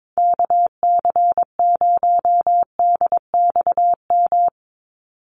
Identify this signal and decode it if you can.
Morse code, heard as a keyed tone, about 22 wpm, 700 Hz. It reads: KC0DXM
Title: USA Call Signs 22wpm